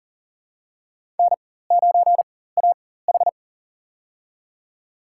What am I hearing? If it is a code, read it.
Morse code, heard as a keyed tone, about 40 wpm, spaced as for 19 wpm, 700 Hz. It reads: N9AH
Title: USA Call Signs - 4 Characters - 4x Character Spacing - 40wpm